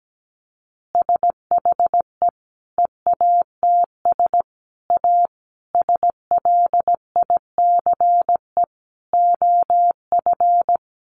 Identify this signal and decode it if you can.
Morse code, heard as a keyed tone, about 17 wpm, 700 Hz. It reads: SHE EATS A SLICE OF
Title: Additional Easy Phrases 17wpm